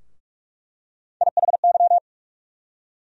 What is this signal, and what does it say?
Morse code, heard as a keyed tone, about 44 wpm, 700 Hz. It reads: IHY
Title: USA Call Sign Postfixes - 3 Letters 45wpm